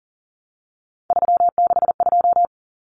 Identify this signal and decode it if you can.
Morse code, heard as a keyed tone, about 37 wpm, 700 Hz. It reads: 362
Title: Common Cut Numbers - 3-Digit 40wpm